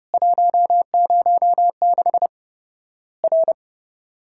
Morse code, heard as a keyed tone, about 32 wpm, 700 Hz. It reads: 106 R
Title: Worked All Europe DX Contest - QTC Exchange - 30wpm